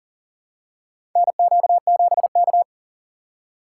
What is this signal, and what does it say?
Morse code, heard as a keyed tone, about 40 wpm, 700 Hz. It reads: NQ7K
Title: USA Call Signs - 2x Character Spacing - 40wpm